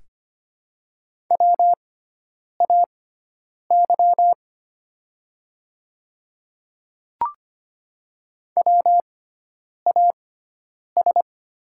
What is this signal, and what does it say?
Morse code, heard as a keyed tone, about 25 wpm, spaced as for 9 wpm, 700 Hz. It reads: WAY WAS
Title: 6x Character Spacing - Top 100 Words - 25wpm